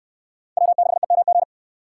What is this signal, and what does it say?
Morse code, heard as a keyed tone, about 60 wpm, 700 Hz. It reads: W6EMG